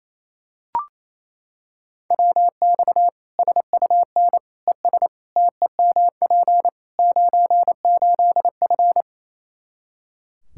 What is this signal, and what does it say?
Morse code, heard as a keyed tone, about 28 wpm, 700 Hz. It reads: WX SUN ES TEMP 98F
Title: K1USN SST QSO Practice 28wpm